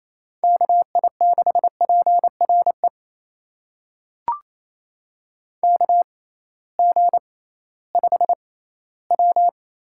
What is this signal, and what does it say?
Morse code, heard as a keyed tone, about 28 wpm, spaced as for 16 wpm, 700 Hz. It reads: KI6PRE KG5W
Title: USA Call Signs - 6 Characters - 6x Character Spacing - 28wpm